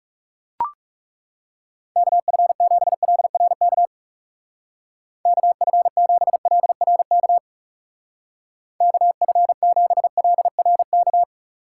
Morse code, heard as a keyed tone, about 36 wpm, 700 Hz. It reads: KF7LRK KF7LRK KF7LRK
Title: Speed Racing - USA 2x3 Call Signs 30wpm